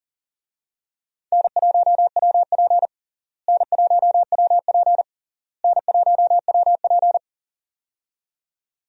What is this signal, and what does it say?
Morse code, heard as a keyed tone, about 40 wpm, 700 Hz. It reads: N1WP N1WP N1WP
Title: USA Call Signs - 4 Characters - Repeated 3x - 40wpm